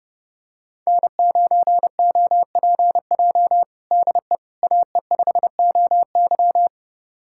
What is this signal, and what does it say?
Morse code, heard as a keyed tone, about 30 wpm, 700 Hz. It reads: N9OPJ DE AE5OY
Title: USA Callsign DE Callsign - 5 Characters - 30wpm